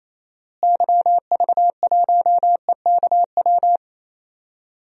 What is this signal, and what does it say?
Morse code, heard as a keyed tone, about 28 wpm, 700 Hz. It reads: YV1EKW
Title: DXCC-100 Call Signs - 6 Characters - Repeated 3x - 28wpm